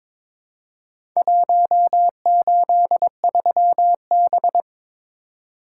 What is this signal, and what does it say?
Morse code, heard as a keyed tone, about 23 wpm, 700 Hz. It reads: 183B
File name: practice / 22wpm / Four Letter-Number Combinations - 22wpm